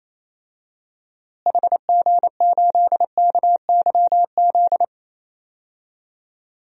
Morse code, heard as a keyed tone, about 28 wpm, 700 Hz. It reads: HG8KYZ
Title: DXCC-100 Call Signs - 6 Characters - Repeated 2x - 28wpm